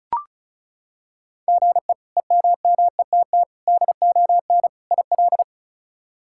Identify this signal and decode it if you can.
Morse code, heard as a keyed tone, about 36 wpm, 700 Hz. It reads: GE EMMETT DON IL
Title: K1USN SST QSO Practice - 35wpm